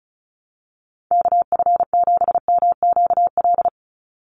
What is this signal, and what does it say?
Morse code, heard as a keyed tone, about 34 wpm, 700 Hz. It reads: KF7MQL
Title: USA Call Signs 35wpm